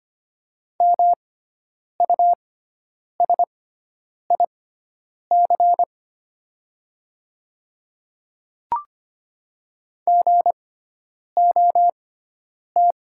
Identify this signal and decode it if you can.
Morse code, heard as a keyed tone, about 25 wpm, spaced as for 9 wpm, 700 Hz. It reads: MUSIC GOT
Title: 6x Character Spacing - Top 300 Words - 25wpm